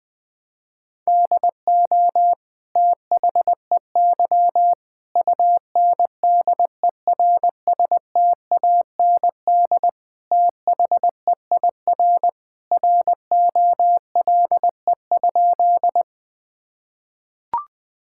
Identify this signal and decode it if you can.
Morse code, heard as a keyed tone, about 20 wpm, 700 Hz. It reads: DO THEY UNDERSTAND THEIR ROLE?
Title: Sentences from Top 1000 Words - Review Part 6 - 20wpm